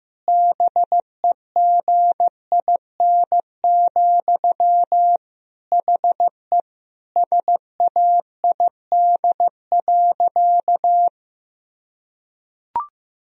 Morse code, heard as a keyed tone, about 15 wpm, 700 Hz. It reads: BEGIN, HE SAID.
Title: Sentences from Top 1000 Words - Review Part 1 - 15wpm